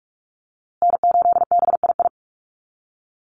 Morse code, heard as a keyed tone, about 44 wpm, 690 Hz. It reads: N8BII
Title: USA Call Signs Repeated 2x 45wpm